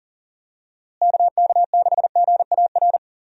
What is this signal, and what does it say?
Morse code, heard as a keyed tone, about 40 wpm, 700 Hz. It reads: KK6GAR
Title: USA Call Signs - 6 Characters - Repeated 2x - 40wpm